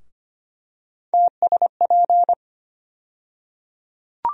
Morse code, heard as a keyed tone, about 25 wpm, 700 Hz. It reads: TSP
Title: USA Call Sign Postfixes - 3 Letters 25wpm